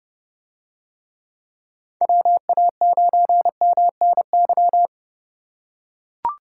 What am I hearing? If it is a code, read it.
Morse code, heard as a keyed tone, about 31 wpm, 700 Hz. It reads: WA9MNY